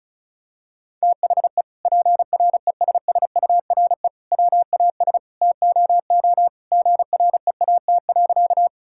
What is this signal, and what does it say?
Morse code, heard as a keyed tone, about 34 wpm, 700 Hz. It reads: THE PRESSURE WAS TOO GREAT.
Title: Sentences from Top 1000 Words - Review Part 6 - 35wpm